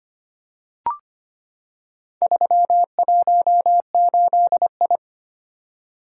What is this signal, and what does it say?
Morse code, heard as a keyed tone, about 25 wpm, 700 Hz. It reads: 318I
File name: practice / 25wpm / Four Letter-Number Combinations - 25wpm